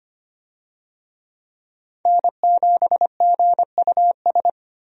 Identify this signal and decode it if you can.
Morse code, heard as a keyed tone, about 25 wpm, 700 Hz. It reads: N7GUS
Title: USA Call Signs - 5 Characters - 25wpm